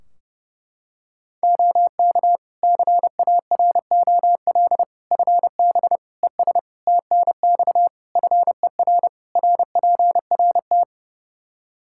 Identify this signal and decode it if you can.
Morse code, heard as a keyed tone, about 32 wpm, 700 Hz. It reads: OK CAROL FB ES TNX FER RPRT
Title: K1USN SST QSO Practice 30wpm